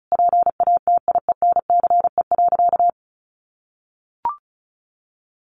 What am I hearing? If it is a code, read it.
Morse code, heard as a keyed tone, about 34 wpm, 700 Hz. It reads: PATIENCE.
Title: Sentences from Top 1300 Words 35wpm